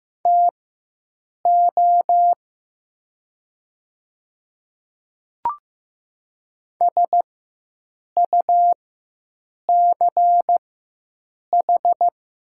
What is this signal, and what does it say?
Morse code, heard as a keyed tone, about 15 wpm, spaced as for 7 wpm, 700 Hz. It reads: TO SUCH
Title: 4x Character Spacing - Top 200 Words - 15wpm